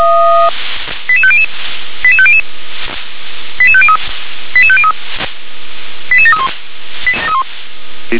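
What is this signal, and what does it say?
Five-tone selective call (ZVEI), heard as quick runs of five tones, tones of about 70 ms. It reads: 88500 88500 88503 88503 88731
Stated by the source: ZVEI five-tone selective call received on SDR.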